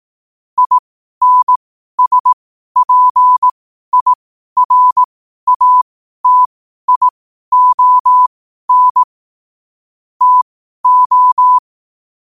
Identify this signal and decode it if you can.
Morse code, heard as a keyed tone, about 18 wpm, spaced as for 12 wpm, 1000 Hz. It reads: INSPIRATION TO
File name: qotdfiles / 12wpm